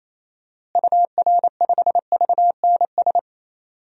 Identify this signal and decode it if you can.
Morse code, heard as a keyed tone, about 27 wpm, 700 Hz. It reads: UR5VNS
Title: DXCC-100 Call Signs - 6 Characters - Repeated 2x - 28wpm